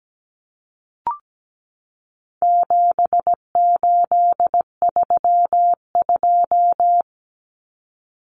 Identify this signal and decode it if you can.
Morse code, heard as a keyed tone, about 17 wpm, 700 Hz. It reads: 7832
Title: Common Cut Numbers - 4-Digit 17wpm